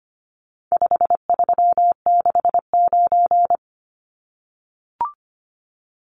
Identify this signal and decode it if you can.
Morse code, heard as a keyed tone, about 25 wpm, 700 Hz. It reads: 5369
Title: Numbers - 4-Digit 25wpm